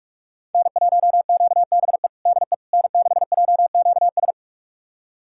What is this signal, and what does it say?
Morse code, heard as a keyed tone, about 41 wpm, 700 Hz. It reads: N1QBE DE N6JQS
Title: USA Callsign DE Callsign - 5 Characters - 45wpm